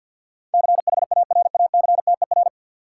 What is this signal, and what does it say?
Morse code, heard as a keyed tone, about 53 wpm, 700 Hz. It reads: CHARACTER
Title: Sentences from Top 1200 Words - New Word Drill - 50wpm